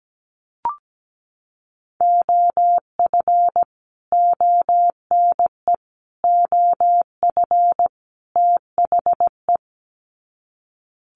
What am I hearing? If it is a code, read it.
Morse code, heard as a keyed tone, about 17 wpm, 700 Hz. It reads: OF ONE OF THE
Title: Sets of 4 Words 17wpm